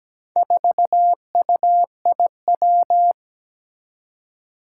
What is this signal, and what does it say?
Morse code, heard as a keyed tone, about 17 wpm, 700 Hz. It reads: 4UIW
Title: Four Challenging Letter-Number Combinations 17wpm